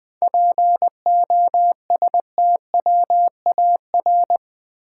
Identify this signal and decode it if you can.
Morse code, heard as a keyed tone, about 20 wpm, 700 Hz. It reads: POSTWAR
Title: 7 Letter Words - First 3 Letters Given - 20wpm